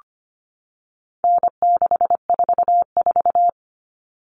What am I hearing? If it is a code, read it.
Morse code, heard as a keyed tone, about 25 wpm, 700 Hz. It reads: N644
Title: All Cut Numbers - 4-Digit 25wpm